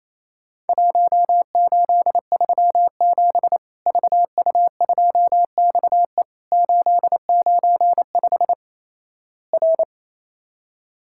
Morse code, heard as a keyed tone, about 28 wpm, 700 Hz. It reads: 1837 VU2XE 895 R
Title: Worked All Europe DX Contest - QTC Exchange - 28wpm